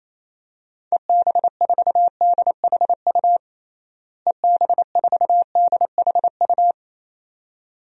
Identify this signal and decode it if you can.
Morse code, heard as a keyed tone, about 28 wpm, 700 Hz. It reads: EB4DHU EB4DHU
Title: DXCC-100 Call Signs - 6 Characters - Repeated 2x - 28wpm